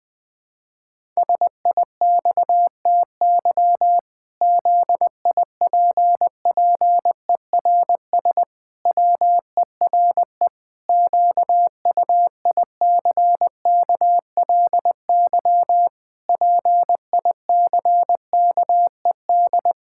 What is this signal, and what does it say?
Morse code, heard as a keyed tone, about 20 wpm, 700 Hz. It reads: SIXTY ZIPPERS WERE QUICKLY PICKED
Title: Pangram Challenge v2 - 20wpm